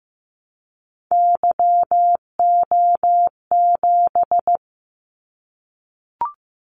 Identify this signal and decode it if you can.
Morse code, heard as a keyed tone, about 15 wpm, 700 Hz. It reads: YO7
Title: Callsign Prefixes for Top 25 DXCC - 15wpm